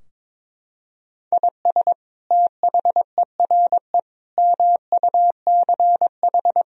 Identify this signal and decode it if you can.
Morse code, heard as a keyed tone, about 22 wpm, 700 Hz. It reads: IS THERE MUCH
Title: Sentences from Top 1000 Words - Review Part 4 - 22wpm